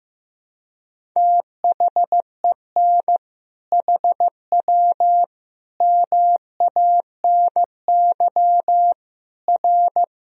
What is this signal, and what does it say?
Morse code, heard as a keyed tone, about 15 wpm, 700 Hz. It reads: THEN HW MANY R
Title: Sentences from Top 100 Words and Abbreviations 15wpm